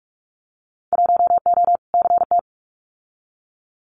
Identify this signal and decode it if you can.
Morse code, heard as a keyed tone, about 46 wpm, 700 Hz. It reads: 1O CT
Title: Winter Field Day Exchanges 45wpm